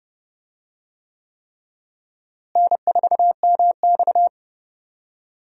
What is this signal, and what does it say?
Morse code, heard as a keyed tone, about 32 wpm, 700 Hz. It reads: N4MX
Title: USA Call Signs - 4 Characters - Repeated 3x - 30wpm